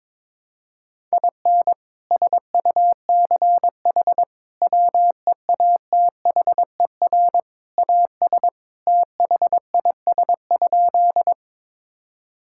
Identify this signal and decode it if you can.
Morse code, heard as a keyed tone, about 22 wpm, 700 Hz. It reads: IN SUCH WEATHER AS THIS?
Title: Sentences from Top 1000 Words - Review Part 5 - 22wpm